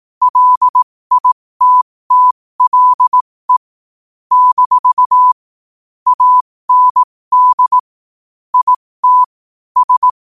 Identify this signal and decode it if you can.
Morse code, heard as a keyed tone, about 18 wpm, spaced as for 15 wpm, 1000 Hz. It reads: LITTLE - AND IT S
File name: qotdfiles / 15wpm